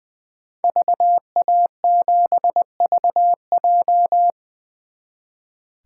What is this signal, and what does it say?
Morse code, heard as a keyed tone, about 21 wpm, 700 Hz. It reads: VA7VJ
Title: 2023 SOTA Chaser Call Signs - 20wpm